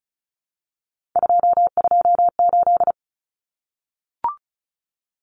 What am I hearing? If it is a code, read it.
Morse code, heard as a keyed tone, about 34 wpm, 700 Hz. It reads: 228
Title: Numbers - 3-Digit 35wpm